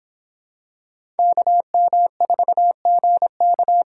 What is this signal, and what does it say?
Morse code, heard as a keyed tone, about 24 wpm, 700 Hz. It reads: KM4GK